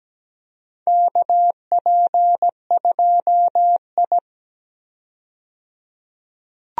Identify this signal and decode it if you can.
Morse code, heard as a keyed tone, about 17 wpm, 700 Hz. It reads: KP2I
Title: DXCC-100 Call Signs - 4 Characters - 17wpm